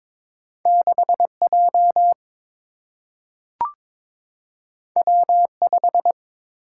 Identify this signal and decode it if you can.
Morse code, heard as a keyed tone, about 22 wpm, 700 Hz. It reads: 6J W5
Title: Single-Letter Single-Number 22wpm